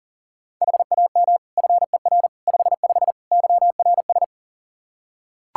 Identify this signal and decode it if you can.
Morse code, heard as a keyed tone, about 40 wpm, 700 Hz. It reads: HAM FER 55 YRS
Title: Practice QSO - Exchange 3 - More Info - 40wpm